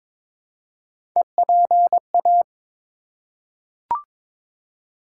Morse code, heard as a keyed tone, about 22 wpm, 700 Hz. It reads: EPA
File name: ARRL Sections v2 - 22wpm